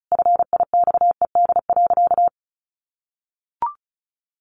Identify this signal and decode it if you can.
Morse code, heard as a keyed tone, about 34 wpm, 700 Hz. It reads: FIXED.
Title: Sentences from Top 1100 Words 35wpm